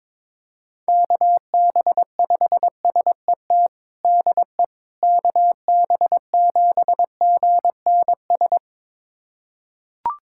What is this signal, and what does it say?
Morse code, heard as a keyed tone, about 21 wpm, 700 Hz. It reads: KB5SET DE KB7GNS
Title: USA Callsign DE Callsign - 6 Characters - 22wpm